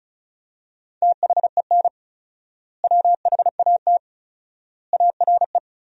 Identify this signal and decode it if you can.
Morse code, heard as a keyed tone, about 34 wpm, 700 Hz. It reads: THEN WHAT ARE
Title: 4x Word-Spacing - Sentences from Top 100 Words - 35wpm